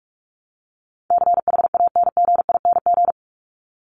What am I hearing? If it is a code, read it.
Morse code, heard as a keyed tone, about 44 wpm, 700 Hz. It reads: CHANGING
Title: ING Suffix - Top 100 Words with Base Words 45wpm